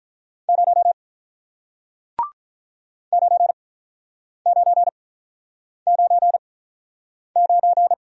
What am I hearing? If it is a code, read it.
Morse code, heard as a keyed tone, about 47 wpm, spaced as for 9 wpm, 700 Hz. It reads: 0 9999